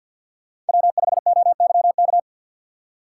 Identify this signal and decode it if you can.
Morse code, heard as a keyed tone, about 52 wpm, 700 Hz. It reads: W5OYK
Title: USA Call Signs - 5 Characters - 2x Character Spacing - 50wpm